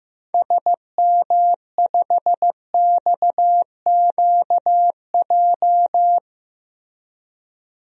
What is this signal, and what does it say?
Morse code, heard as a keyed tone, about 15 wpm, 700 Hz. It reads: SM5XQJ
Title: DXCC-100 Call Signs - 6 Characters - 15wpm